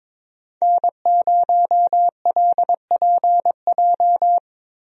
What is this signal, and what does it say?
Morse code, heard as a keyed tone, about 22 wpm, 700 Hz. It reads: N0LPJ
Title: USA Call Signs - 5 Characters - Repeated 2x - 22wpm